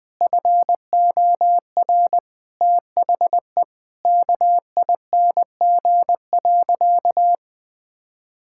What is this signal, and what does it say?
Morse code, heard as a keyed tone, about 20 wpm, 700 Hz. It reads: FOR THE KING.
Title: Sentences from Top 1000 Words - Review Part 6 - 20wpm